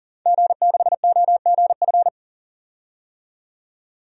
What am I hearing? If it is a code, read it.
Morse code, heard as a keyed tone, about 40 wpm, 700 Hz. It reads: G6OGF
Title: DXCC-100 Call Signs - 5 Characters - 40wpm